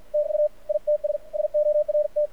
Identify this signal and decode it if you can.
Morse code, heard as a keyed tone, about 56 wpm, 600 Hz. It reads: TU ITS SORT
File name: archived / CRSnet